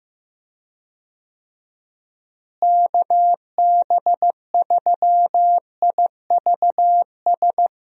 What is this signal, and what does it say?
Morse code, heard as a keyed tone, about 15 wpm, 700 Hz. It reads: KB3IVS
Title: USA Call Signs - 6 Characters - Repeated 2x - 15wpm